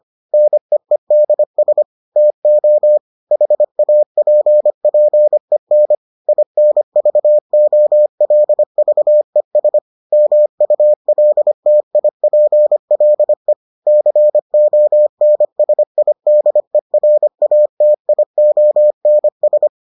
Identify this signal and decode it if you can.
Morse code, heard as a keyed tone, about 25 wpm, 600 Hz. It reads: NEEDS TO HAPPEN INVOLVES MULTIPLE CONSIDERATIONS